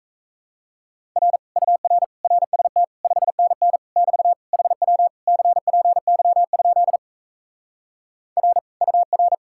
Morse code, heard as a keyed tone, about 37 wpm, 700 Hz. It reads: R UR RST 5NN <BT> HW CPY? R UR